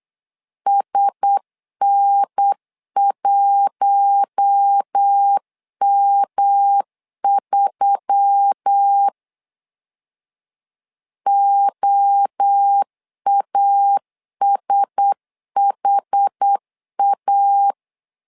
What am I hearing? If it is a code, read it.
Morse code, heard as a keyed tone, about 8 wpm, 800 Hz. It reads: SN1M3 OASHA
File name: LEZIONI / 4 Q1S3 8wpm 3037